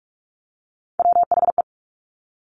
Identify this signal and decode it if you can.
Morse code, heard as a keyed tone, about 46 wpm, 700 Hz. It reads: WHE
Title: Top Three Letter Combinations in Words 45wpm